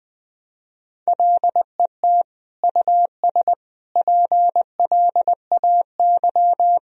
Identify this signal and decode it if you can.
Morse code, heard as a keyed tone, about 21 wpm, 700 Hz. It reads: LET US PLAY
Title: Sentences from Top 1000 Words - Review Part 3 - 20wpm